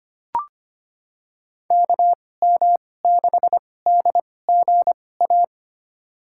Morse code, heard as a keyed tone, about 25 wpm, spaced as for 18 wpm, 700 Hz. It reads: KM6DGA
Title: USA Call Signs - 6 Characters - 2x Character Spacing - 25wpm